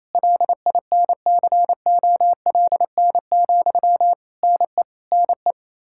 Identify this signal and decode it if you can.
Morse code, heard as a keyed tone, about 27 wpm, 700 Hz. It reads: LINCOLN, NE NE
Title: Practice QSO - Exchange 1 - Beginning - 28wpm